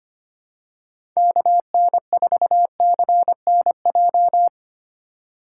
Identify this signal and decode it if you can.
Morse code, heard as a keyed tone, about 25 wpm, 700 Hz. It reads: KN4CNJ
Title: USA Call Signs - 6 Characters - 3x Character Spacing - 25wpm